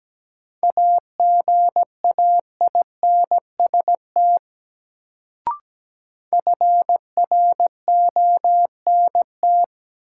Morse code, heard as a keyed tone, about 17 wpm, 700 Hz. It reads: AGAINST FRONT
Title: Sets of 2 Words - New Word Drill - 17wpm